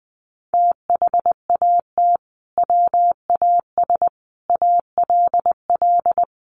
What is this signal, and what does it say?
Morse code, heard as a keyed tone, about 20 wpm, 700 Hz. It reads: THAT WAS ALL
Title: Sentences from Top 1100 Words 20wpm